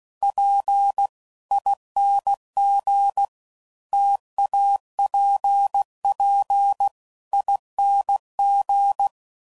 Morse code, heard as a keyed tone, about 16 wpm, 800 Hz. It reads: PING TAPPING
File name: Combinations / Difficult Combos / T AOEI's